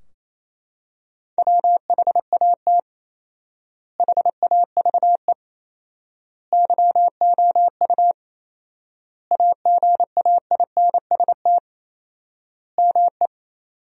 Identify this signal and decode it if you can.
Morse code, heard as a keyed tone, about 28 wpm, 700 Hz. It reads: WHAT HAVE YOU AGAINST ME
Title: 4x Word-Spacing - Sentences from Top 400 Words - 28wpm